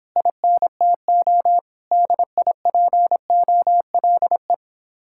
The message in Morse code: INTO DIPOLE

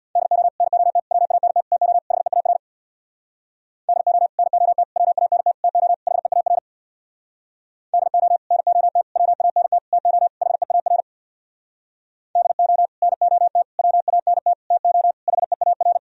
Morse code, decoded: DO NOT WANT TO HEAR DO NOT WANT TO HEAR DO NOT WANT TO HEAR DO NOT WANT TO HEAR

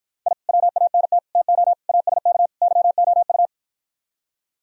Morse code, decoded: I WANT TO ASK YOU

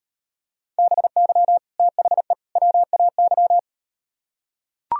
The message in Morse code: BY THE WAY

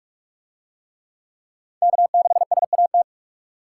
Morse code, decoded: K6SAT

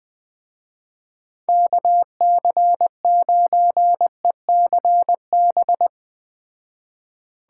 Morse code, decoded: KC9ECB

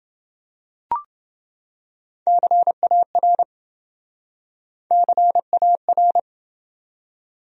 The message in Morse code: CAR CAR

